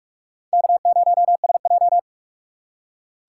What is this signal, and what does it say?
Morse code, heard as a keyed tone, about 42 wpm, 700 Hz. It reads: K0SJ
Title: USA Call Signs - 4 Characters - 6x Character Spacing - 45wpm